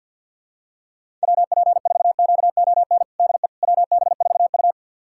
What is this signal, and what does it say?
Morse code, heard as a keyed tone, about 52 wpm, 700 Hz. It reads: WP4QON DE WB4U
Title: USA Callsign DE Callsign 50wpm